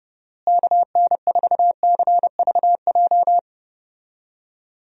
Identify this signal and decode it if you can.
Morse code, heard as a keyed tone, about 30 wpm, 700 Hz. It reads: KN4CVJ
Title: USA Call Signs - 6 Characters - Repeated 2x - 30wpm